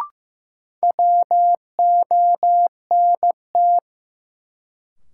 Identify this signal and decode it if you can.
Morse code, heard as a keyed tone, about 15 wpm, 700 Hz. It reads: WONT
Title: Sentences from Top 1000 Words - Review Part 6 - New Word Drill - 15wpm